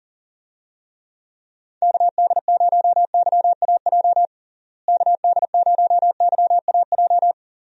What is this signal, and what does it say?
Morse code, heard as a keyed tone, about 40 wpm, 700 Hz. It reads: KD0YAJ KD0YAJ
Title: USA Call Signs - 6 Characters - Repeated 3x - 40wpm